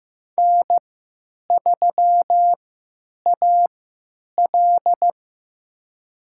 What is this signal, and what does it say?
Morse code, heard as a keyed tone, about 15 wpm, spaced as for 9 wpm, 700 Hz. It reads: N3AL
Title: USA Call Signs - 3x Character Spacing - 15wpm